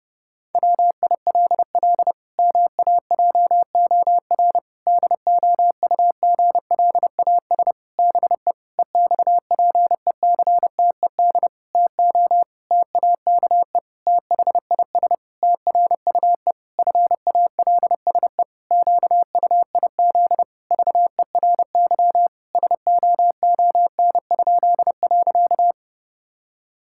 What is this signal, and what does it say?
Morse code, heard as a keyed tone, about 30 wpm, 700 Hz. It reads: WILL MAJOR DOUGLAS BE EXPECTED TO TAKE THIS TRUE FALSE QUIZ VERY SOON?.
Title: Pangram Challenge v2 - 30wpm